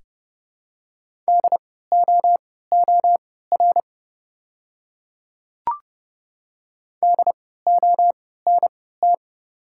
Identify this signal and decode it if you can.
Morse code, heard as a keyed tone, about 30 wpm, spaced as for 17 wpm, 690 Hz. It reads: DOOR DONT
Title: 3x Character Spacing - Top 300 Words - 30wpm